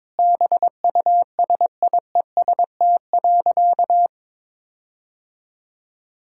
Morse code, heard as a keyed tone, about 22 wpm, 700 Hz. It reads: BUSIEST.
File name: Sentences with 7-Word Phrases - 22wpm